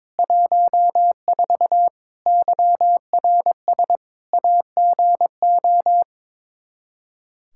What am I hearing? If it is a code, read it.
Morse code, heard as a keyed tone, about 22 wpm, 700 Hz. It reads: 14 YRS AGO
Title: Practice QSO - Exchange 3 - More Info - 22wpm